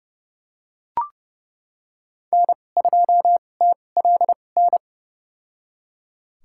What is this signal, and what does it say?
Morse code, heard as a keyed tone, about 30 wpm, spaced as for 22 wpm, 700 Hz. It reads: N2TLN
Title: USA Call Signs - 5 Characters - 2x Character Spacing - 30wpm